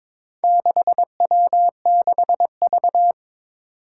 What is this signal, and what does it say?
Morse code, heard as a keyed tone, about 22 wpm, 700 Hz. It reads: 6W6V
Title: Four Challenging Letter-Number Combinations 22wpm